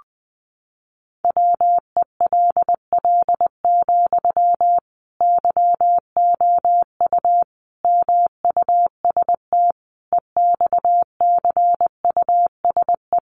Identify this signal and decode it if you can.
Morse code, heard as a keyed tone, about 20 wpm, 700 Hz. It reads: WELL, YOU MUST EXCUSE
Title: Sentences from Top 700 Words 20wpm